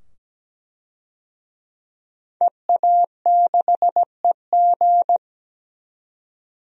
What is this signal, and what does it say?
Morse code, heard as a keyed tone, about 17 wpm, 700 Hz. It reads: EA6EG
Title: DXCC-100 Call Signs - 5 Characters - 17wpm